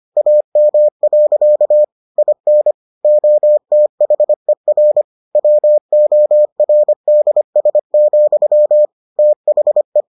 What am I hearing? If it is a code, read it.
Morse code, heard as a keyed tone, about 25 wpm, 600 Hz. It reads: AM. IN OTHER WORDS, THE